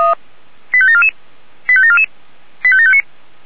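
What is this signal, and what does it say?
Five-tone selective call (ZVEI), heard as quick runs of five tones, tones of about 70 ms. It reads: 85744 85744 85759
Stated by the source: ZVEI five-tone call received by SDR.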